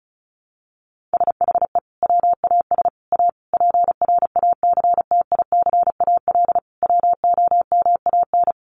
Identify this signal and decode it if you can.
Morse code, heard as a keyed tone, about 34 wpm, 700 Hz. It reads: SHE WAS A PRACTICAL WOMAN
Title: Sentences from Top 1400 Words 35wpm